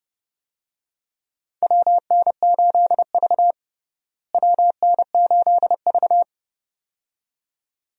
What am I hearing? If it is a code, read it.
Morse code, heard as a keyed tone, about 32 wpm, 700 Hz. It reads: WN8V WN8V
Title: USA Call Signs - 4 Characters - Repeated 2x - 30wpm